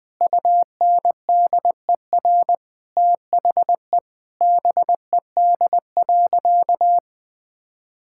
Morse code, heard as a keyed tone, about 20 wpm, 700 Hz. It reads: UNDER THE BED.
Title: Sentences from Top 1000 Words - Review Part 6 - 20wpm